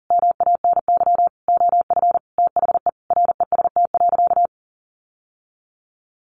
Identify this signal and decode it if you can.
Morse code, heard as a keyed tone, about 40 wpm, 700 Hz. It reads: MANY OF THE REST.
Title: Sentences from Top 1500 Words 40wpm